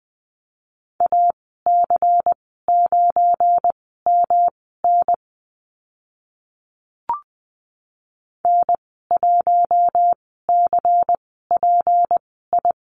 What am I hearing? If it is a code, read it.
Morse code, heard as a keyed tone, about 20 wpm, spaced as for 14 wpm, 700 Hz. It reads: AC9MN N1CPI
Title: 193 Lesson 31g - Letter v - Call Signs 20wpm